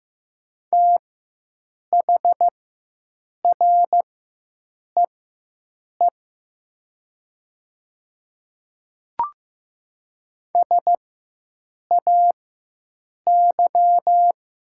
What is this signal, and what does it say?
Morse code, heard as a keyed tone, about 15 wpm, spaced as for 7 wpm, 700 Hz. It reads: THREE SAY
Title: 4x Character Spacing - Top 300 Words - 15wpm